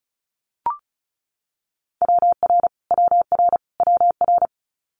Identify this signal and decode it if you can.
Morse code, heard as a keyed tone, about 36 wpm, 700 Hz. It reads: WR WR WR
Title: Multi-letter Phonemes Repeated 3x 35wpm